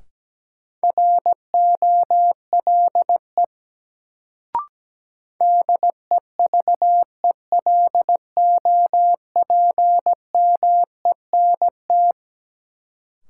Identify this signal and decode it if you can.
Morse code, heard as a keyed tone, about 17 wpm, 700 Hz. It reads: ROLE DEVELOPMENT
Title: Sets of 3 Words - New Word Drill - 17wpm